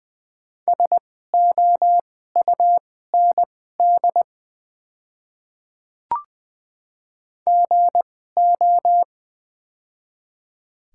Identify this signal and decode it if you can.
Morse code, heard as a keyed tone, about 20 wpm, spaced as for 15 wpm, 700 Hz. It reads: SOUND GO
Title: 2x Character Spacing - Top 100 Words - 20wpm